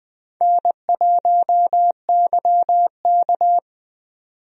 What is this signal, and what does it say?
Morse code, heard as a keyed tone, about 20 wpm, 700 Hz. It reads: N1YK
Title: USA Call Signs - 4 Characters - 20wpm